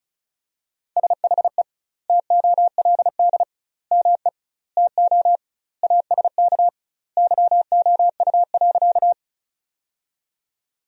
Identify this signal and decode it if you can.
Morse code, heard as a keyed tone, about 36 wpm, 700 Hz. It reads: SHE TOLD ME TO ASK YOU.
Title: 2x Word-Spacing - Sentences from Top 400 Words - 35wpm